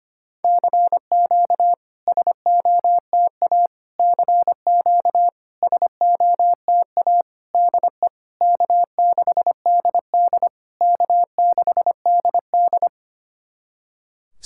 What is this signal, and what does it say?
Morse code, heard as a keyed tone, about 25 wpm, 700 Hz. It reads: CQ SOTA CQ SOTA DE K6DD K6DD
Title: SOTA QSO Practice 25wpm